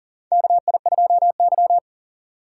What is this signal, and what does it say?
Morse code, heard as a keyed tone, about 38 wpm, 700 Hz. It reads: KI2Y
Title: USA Call Signs - 4 Characters - 4x Character Spacing - 40wpm